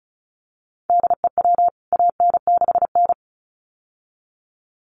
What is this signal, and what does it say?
Morse code, heard as a keyed tone, about 34 wpm, 700 Hz. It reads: DEW AN6N
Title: CWT Exchange 35wpm